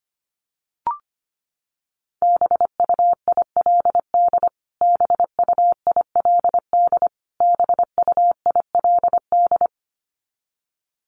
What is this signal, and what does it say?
Morse code, heard as a keyed tone, about 25 wpm, 700 Hz. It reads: BUILD BUILD BUILD
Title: Top 300 Words Repeated 25wpm